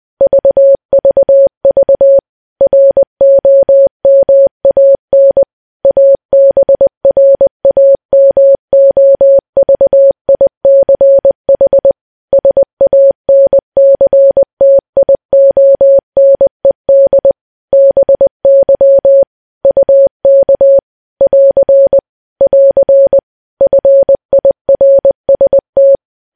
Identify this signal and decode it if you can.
Morse code, heard as a keyed tone, about 20 wpm, 550 Hz. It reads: VVV ROMAN ABRAMOVICH SANCTIONED BY UK <AR> <AR> FIRST
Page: news-headlines-in-morse-code-at-20-wpm